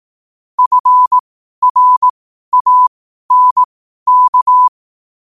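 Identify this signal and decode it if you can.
Morse code, heard as a keyed tone, about 18 wpm, spaced as for 12 wpm, 1000 Hz. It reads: FRANK
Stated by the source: Morse question-of-the-day sent at 12 wpm.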